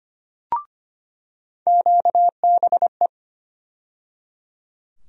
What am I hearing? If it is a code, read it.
Morse code, heard as a keyed tone, about 25 wpm, 700 Hz. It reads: QBE
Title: USA Call Sign Postfixes - 3 Letters 25wpm